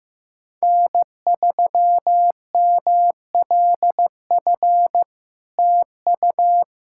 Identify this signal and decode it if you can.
Morse code, heard as a keyed tone, about 15 wpm, 700 Hz. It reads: N3MLF TU
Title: SOTA QSO Practice 15wpm